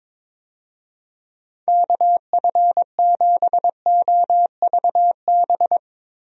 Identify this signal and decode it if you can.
Morse code, heard as a keyed tone, about 22 wpm, 700 Hz. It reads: KF7OVB
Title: USA Call Signs - 6 Characters - Repeated 2x - 22wpm